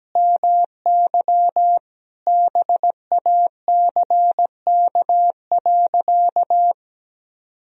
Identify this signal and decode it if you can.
Morse code, heard as a keyed tone, about 17 wpm, 700 Hz. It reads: MY BACK.
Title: Sentences from Top 1000 Words - Review Part 3 - 17wpm